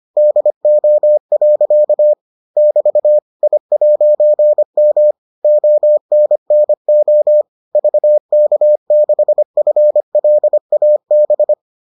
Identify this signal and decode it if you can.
Morse code, heard as a keyed tone, about 25 wpm, 600 Hz. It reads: DO. <BT> I'M ONNO VK6FLAB